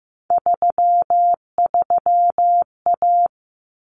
Morse code, heard as a keyed tone, about 15 wpm, 700 Hz. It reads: 33A
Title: Common Cut Numbers - 3-Digit 15wpm